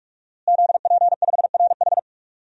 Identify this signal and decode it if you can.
Morse code, heard as a keyed tone, about 46 wpm, 700 Hz. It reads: ZP5RH
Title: DXCC-100 Call Signs - 5 Characters - Repeated 3x - 45wpm